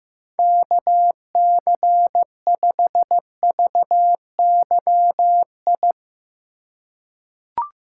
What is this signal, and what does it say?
Morse code, heard as a keyed tone, about 15 wpm, 700 Hz. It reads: KC5VYI
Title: USA Call Signs - 6 Characters - 2x Character Spacing - 15wpm